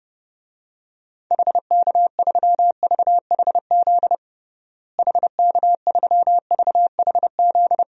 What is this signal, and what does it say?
Morse code, heard as a keyed tone, about 32 wpm, 700 Hz. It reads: HK3VHZ HK3VHZ
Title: DXCC-100 Call Signs - 6 Characters - Repeated 2x - 30wpm